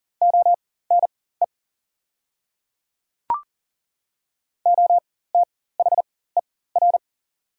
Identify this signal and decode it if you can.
Morse code, heard as a keyed tone, about 40 wpm, spaced as for 19 wpm, 690 Hz. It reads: ONE OTHER
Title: 4x Character Spacing - Top 100 Words - 40wpm